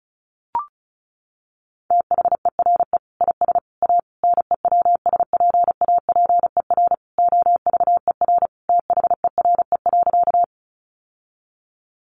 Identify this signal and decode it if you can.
Morse code, heard as a keyed tone, about 34 wpm, 700 Hz. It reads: THERE IS A NEWSPAPER OVER THERE.
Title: Sentences from Top 1100 Words 35wpm